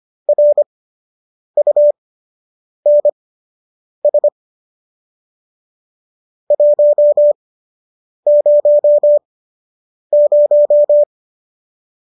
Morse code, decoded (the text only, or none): RUNS 100